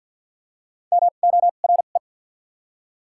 Morse code, heard as a keyed tone, about 54 wpm, 700 Hz. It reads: M O R E
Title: 2x Character Spacing - Top 100 Words - 50wpm